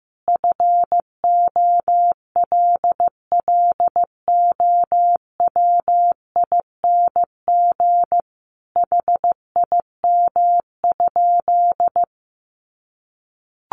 Morse code, heard as a keyed tone, about 15 wpm, 700 Hz. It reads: FOLLOWING HIM?
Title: Sentences from Top 700 Words 15wpm